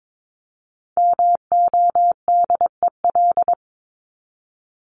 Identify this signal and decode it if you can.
Morse code, heard as a keyed tone, about 22 wpm, 700 Hz. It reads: MODEL
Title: Top 501-750 Words 22wpm